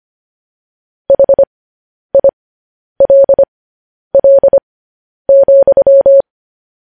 Morse code, heard as a keyed tone, about 25 wpm, spaced as for 10 wpm, 550 Hz. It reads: HILL,
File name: podcast / News headlines-1x-25x10